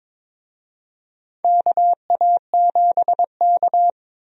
Morse code, heard as a keyed tone, about 22 wpm, 700 Hz. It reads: KA7K
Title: USA Call Signs - 4 Characters - 22wpm